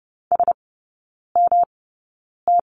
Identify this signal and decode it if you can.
Morse code, heard as a keyed tone, about 30 wpm, spaced as for 9 wpm, 700 Hz. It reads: SMT
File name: Lesson 18e - Letter m - Three Characters 30wpm